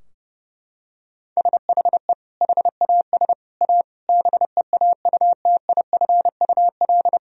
Morse code, heard as a keyed tone, about 30 wpm, 700 Hz. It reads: SHE HAS A BEAUTIFUL
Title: Sentences from Top 1000 Words - Review Part 2 - 30wpm